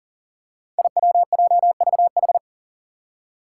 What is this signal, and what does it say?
Morse code, heard as a keyed tone, about 40 wpm, 700 Hz. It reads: IWJVH
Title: Five Challenging Letter-Number Combinations 40wpm